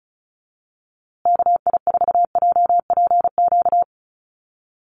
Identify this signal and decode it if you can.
Morse code, heard as a keyed tone, about 34 wpm, 700 Hz. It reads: KI4JPQ
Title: USA Call Signs Repeated 3x 35wpm